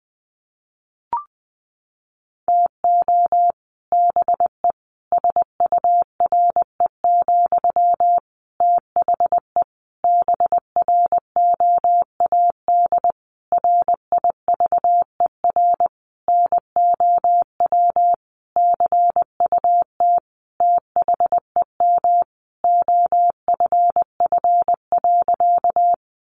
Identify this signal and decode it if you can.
Morse code, heard as a keyed tone, about 20 wpm, 700 Hz. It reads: TO BE SURE, THE BROAD RIVER NOW CUT THEM OFF.
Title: Sentences from Top 1000 Words 20wpm